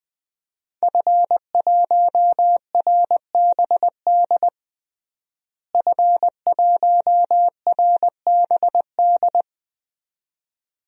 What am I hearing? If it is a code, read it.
Morse code, heard as a keyed tone, about 20 wpm, 700 Hz. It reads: F1RBD F1RBD
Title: DXCC-100 Call Signs - 5 Characters - Repeated 2x - 20wpm